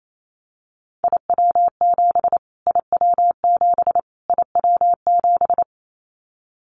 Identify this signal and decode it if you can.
Morse code, heard as a keyed tone, about 28 wpm, 700 Hz. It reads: IW7 IW7 IW7
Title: Callsign Prefixes for Top 25 DXCC - Repeated 3x - 28wpm